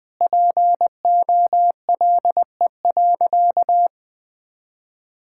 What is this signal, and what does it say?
Morse code, heard as a keyed tone, about 20 wpm, 700 Hz. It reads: POLE.
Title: Sentences with 5-Word Phrases - 20wpm